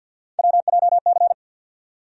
Morse code, heard as a keyed tone, about 54 wpm, 700 Hz. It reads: WJC
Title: USA Call Sign Postfixes - 3 Letters 50wpm